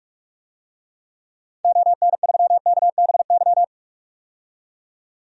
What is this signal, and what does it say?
Morse code, heard as a keyed tone, about 44 wpm, 690 Hz. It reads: ON3KBY
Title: DXCC-100 Call Signs - 6 Characters - 45wpm